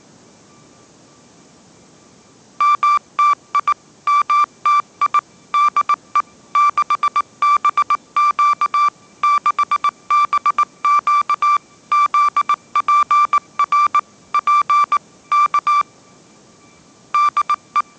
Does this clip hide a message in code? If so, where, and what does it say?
Yes: Morse code from 2.6 to 17.8 s, heard as a keyed tone, about 19 wpm, 1170 Hz. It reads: MTI MTI DE 6BQ 6BQ ZPR P K DE